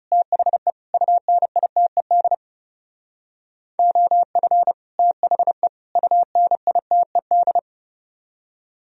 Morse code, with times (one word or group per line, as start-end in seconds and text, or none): 0.1-0.7 THE
0.9-2.3 UNITED
3.8-4.7 OF
5.0-5.7 THE
6.0-7.6 UNITED